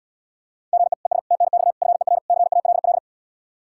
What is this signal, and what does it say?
Morse code, heard as a keyed tone, about 77 wpm, 700 Hz. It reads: GE ES TNX FER CALL